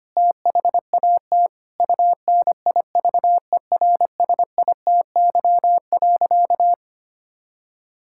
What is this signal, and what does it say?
Morse code, heard as a keyed tone, about 25 wpm, 700 Hz. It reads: THAT UNIVERSITY.
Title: Sentences from Top 1000 Words - Review Part 6 - 25wpm